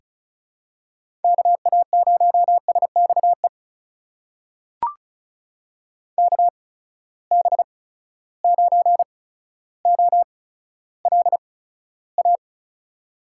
Morse code, heard as a keyed tone, about 36 wpm, spaced as for 15 wpm, 700 Hz. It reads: KA0SXE KB9OLA
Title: USA Call Signs - 6 Characters - 8x Character Spacing - 35wpm